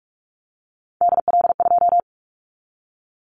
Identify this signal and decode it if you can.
Morse code, heard as a keyed tone, about 46 wpm, 690 Hz. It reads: DL2
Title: Callsign Prefixes for Top 50 DXCC - 45wpm